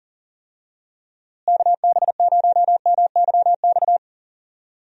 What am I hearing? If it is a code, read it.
Morse code, heard as a keyed tone, about 38 wpm, 700 Hz. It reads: KB0MYX
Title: USA Call Signs - 6 Characters - Repeated 3x - 40wpm